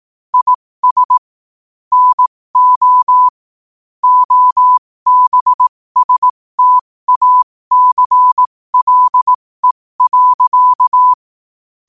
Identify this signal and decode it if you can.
Morse code, heard as a keyed tone, about 18 wpm, spaced as for 15 wpm, 1000 Hz. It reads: IS NO OBSTACLE.
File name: qotdfiles / 15wpm